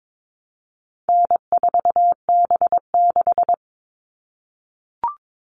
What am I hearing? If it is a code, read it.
Morse code, heard as a keyed tone, about 22 wpm, 700 Hz. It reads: N4B6
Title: All Cut Numbers - 4-Digit 22wpm